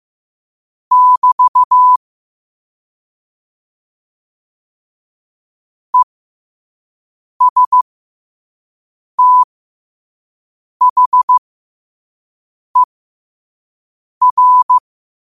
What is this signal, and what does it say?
Morse code, heard as a keyed tone, about 15 wpm, spaced as for 5 wpm, 1000 Hz. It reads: <BT> ESTHER
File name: qotdfiles / 5wpm